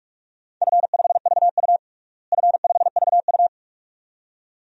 Morse code, heard as a keyed tone, about 46 wpm, 700 Hz. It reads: F5VU F5VU
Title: DXCC-100 Call Signs - 4 Characters - Repeated 2x - 45wpm